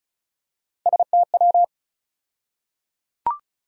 Morse code, heard as a keyed tone, about 36 wpm, 700 Hz. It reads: STW
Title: USA Call Sign Postfixes - 3 Letters 35wpm